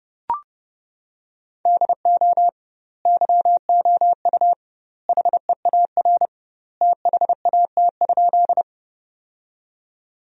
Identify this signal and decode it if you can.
Morse code, heard as a keyed tone, about 30 wpm, 700 Hz. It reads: DO YOU HEAR THAT?
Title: 2x Word-Spacing - Sentences from Top 300 Words - 30wpm